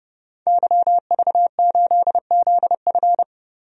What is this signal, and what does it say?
Morse code, heard as a keyed tone, about 30 wpm, 700 Hz. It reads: YV8ZF
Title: DXCC-100 Call Signs - 5 Characters - 30wpm